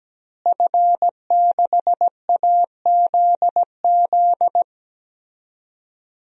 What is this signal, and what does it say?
Morse code, heard as a keyed tone, about 17 wpm, 700 Hz. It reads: F6AZZ
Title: DXCC-100 Call Signs - 5 Characters - Repeated 3x - 17wpm